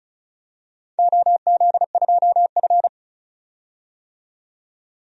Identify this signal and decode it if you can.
Morse code, heard as a keyed tone, about 36 wpm, 700 Hz. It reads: OZ2F
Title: DXCC-100 Call Signs - 4 Characters - 35wpm